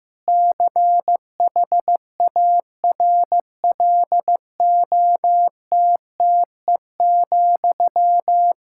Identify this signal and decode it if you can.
Morse code, heard as a keyed tone, about 15 wpm, 700 Hz. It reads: CHARLOTTE,
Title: Practice QSO - Exchange 1 - Beginning - 15wpm